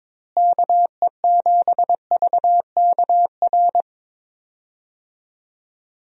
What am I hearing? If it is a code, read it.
Morse code, heard as a keyed tone, about 22 wpm, 700 Hz. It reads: KE7VKR
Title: USA Call Signs - 6 Characters - Repeated 3x - 22wpm